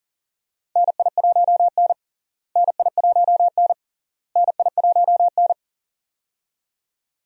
Morse code, heard as a keyed tone, about 40 wpm, 700 Hz. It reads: NI1N NI1N NI1N
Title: USA Call Signs - 4 Characters - Repeated 3x - 40wpm